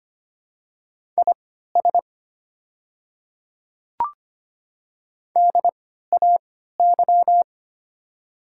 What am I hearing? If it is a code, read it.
Morse code, heard as a keyed tone, about 25 wpm, spaced as for 14 wpm, 690 Hz. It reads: IS DAY
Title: 3x Character Spacing - Top 100 Words - 25wpm